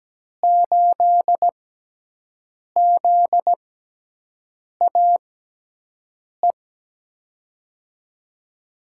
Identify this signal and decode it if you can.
Morse code, heard as a keyed tone, about 17 wpm, 700 Hz. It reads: 8 Z A E
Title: USA Call Signs - 6 Characters - 6x Character Spacing - 17wpm